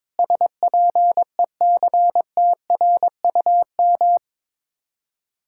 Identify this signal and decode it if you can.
Morse code, heard as a keyed tone, about 22 wpm, 700 Hz. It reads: SPECTRUM
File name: practice / 22wpm / Sets of 7 Words - 22wpm